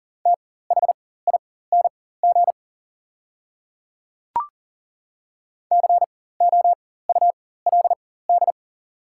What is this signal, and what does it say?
Morse code, heard as a keyed tone, about 39 wpm, spaced as for 19 wpm, 700 Hz. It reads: THING COULD
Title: 4x Character Spacing - Top 100 Words - 40wpm